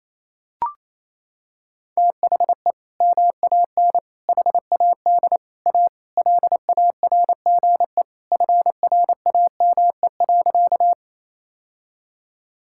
Morse code, heard as a keyed tone, about 28 wpm, 700 Hz. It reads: THE MAN HAD A LARGE FRAME.
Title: Sentences from Top 1000 Words - Review Part 6 - 28wpm